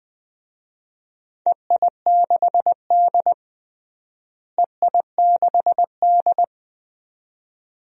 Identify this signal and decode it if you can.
Morse code, heard as a keyed tone, about 21 wpm, 700 Hz. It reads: EI6D EI6D
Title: DXCC-100 Call Signs - 4 Characters - Repeated 2x - 20wpm